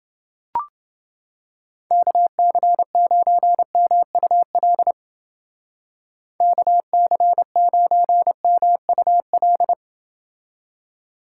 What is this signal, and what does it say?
Morse code, heard as a keyed tone, about 30 wpm, 700 Hz. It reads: KC9MUL KC9MUL